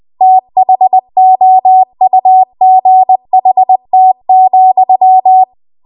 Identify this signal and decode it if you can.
Morse code, heard as a keyed tone, about 20 wpm, 750 Hz. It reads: THOUGHT,
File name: Archive / 20 WPM / 20WPM